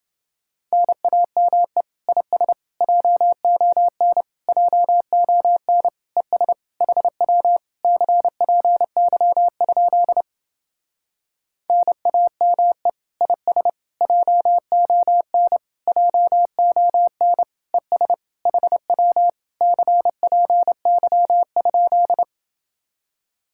Morse code, heard as a keyed tone, about 28 wpm, 700 Hz. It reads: NAME IS JON JON ES HW CPY? NAME IS JON JON ES HW CPY?